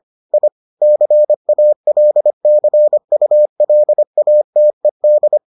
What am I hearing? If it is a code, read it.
Morse code, heard as a keyed tone, about 25 wpm, 600 Hz. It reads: I CALCULATED